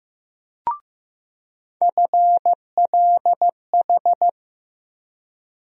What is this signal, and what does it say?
Morse code, heard as a keyed tone, about 16 wpm, 700 Hz. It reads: FLH
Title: USA Call Sign Postfixes - 3 Letters 15wpm